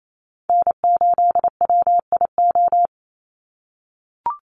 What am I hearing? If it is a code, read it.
Morse code, heard as a keyed tone, about 28 wpm, 700 Hz. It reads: N8WIO
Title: USA Call Signs 28wpm